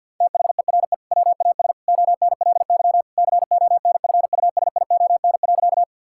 Morse code, heard as a keyed tone, about 48 wpm, 700 Hz. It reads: THERE WAS ONLY CONFUSION.
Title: Sentences from Top 1600 Words 50wpm